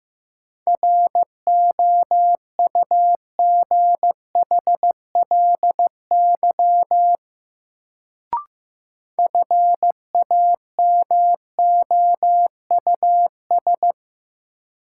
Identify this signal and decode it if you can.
Morse code, heard as a keyed tone, about 15 wpm, 700 Hz. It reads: ROUGHLY FAMOUS
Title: Sentences from Top 1500 Words - New Word Drill - 15wpm